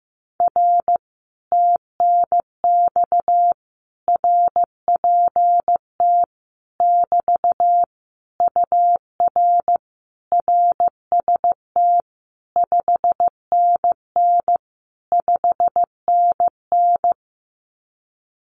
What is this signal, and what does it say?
Morse code, heard as a keyed tone, about 15 wpm, 700 Hz. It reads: R TNX RPT <BT> UR RST 5NN 5NN
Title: Common QSO Phrases Encore 15wpm